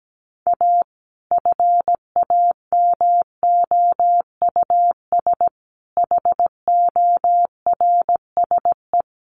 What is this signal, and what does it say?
Morse code, heard as a keyed tone, about 17 wpm, 700 Hz. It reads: A FAMOUS HORSE
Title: Sentences from Top 1500 Words 17wpm